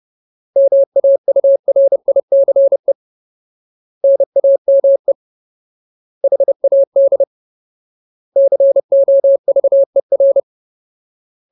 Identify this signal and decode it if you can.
Morse code, heard as a keyed tone, about 30 wpm, 550 Hz. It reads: MAURICE NAME HAD COVER
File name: Ultimate Word List / 30WPM Ultimate